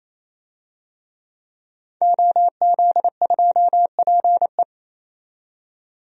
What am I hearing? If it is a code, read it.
Morse code, heard as a keyed tone, about 28 wpm, 700 Hz. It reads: OZ2PE
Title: DXCC-100 Call Signs - 5 Characters - 28wpm